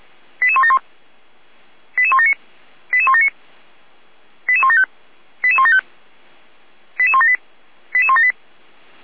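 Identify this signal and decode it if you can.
Five-tone selective call (ZVEI), heard as quick runs of five tones, tones of about 80 ms. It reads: 88171 88179 88179 88175 88175 88178 88178
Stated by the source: ZVEI five-tone call received by SDR, from the fire brigade archive of Feuerwehr Orlamünde.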